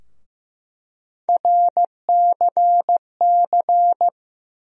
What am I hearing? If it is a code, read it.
Morse code, heard as a keyed tone, about 15 wpm, 700 Hz. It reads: RCC
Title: USA Call Sign Postfixes - 3 Letters 15wpm